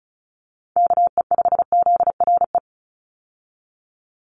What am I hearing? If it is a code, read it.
Morse code, heard as a keyed tone, about 34 wpm, 700 Hz. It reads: KE5ZRE
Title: USA Call Signs 35wpm